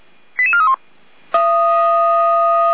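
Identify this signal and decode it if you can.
Five-tone selective call (ZVEI), heard as quick runs of five tones, tones of about 70 ms. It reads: 88431
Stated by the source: ZVEI five-tone selective call received on SDR.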